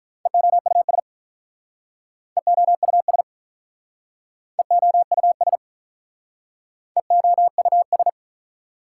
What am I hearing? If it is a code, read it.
Morse code, heard as a keyed tone, about 46 wpm, 690 Hz. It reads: EOUS EOUS EOUS EOUS